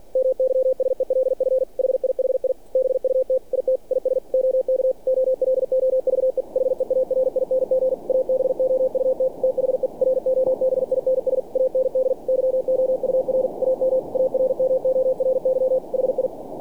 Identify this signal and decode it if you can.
Morse code, heard as a keyed tone, about 48 wpm, 520 Hz. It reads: MYSELF HIHI BUT IT IS OK OLOVE HEARING ABOUT THE ROBINS AND YOUR AM ARMORY HI
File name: archived / CRSnet